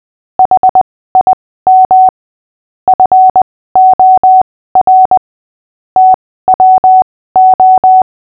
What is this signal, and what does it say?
Morse code, heard as a keyed tone, about 20 wpm, spaced as for 15 wpm, 750 Hz. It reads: HIM FOR TWO
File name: cw / fists / Short-Story-5-15wpm